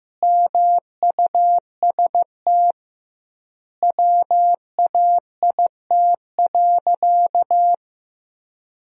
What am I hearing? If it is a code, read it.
Morse code, heard as a keyed tone, about 15 wpm, 700 Hz. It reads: MUST WAIT.
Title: 2x Word-Spacing - Sentences from Top 500 Words - 15wpm